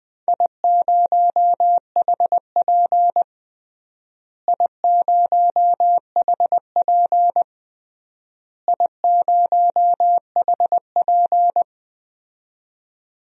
Morse code, heard as a keyed tone, about 20 wpm, 700 Hz. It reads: I0HP I0HP I0HP
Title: DXCC-100 Call Signs - 4 Characters - Repeated 3x - 20wpm